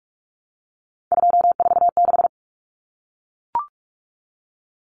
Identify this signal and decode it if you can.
Morse code, heard as a keyed tone, about 44 wpm, 700 Hz. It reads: 246
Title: Numbers - 3-Digit 45wpm